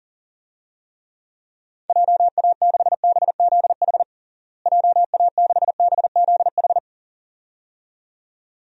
Morse code, heard as a keyed tone, about 37 wpm, 700 Hz. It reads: JA6BZH JA6BZH
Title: DXCC-100 Call Signs - 6 Characters - Repeated 2x - 40wpm